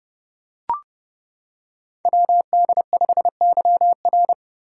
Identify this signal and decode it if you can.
Morse code, heard as a keyed tone, about 30 wpm, 700 Hz. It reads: WD5YR
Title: Code Groups - 5 Letter:Numbers - 30wpm